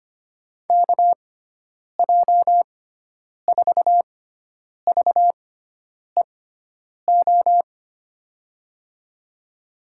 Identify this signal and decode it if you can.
Morse code, heard as a keyed tone, about 25 wpm, spaced as for 9 wpm, 700 Hz. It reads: KJ4VEO
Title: USA Call Signs - 6x Character Spacing - 25wpm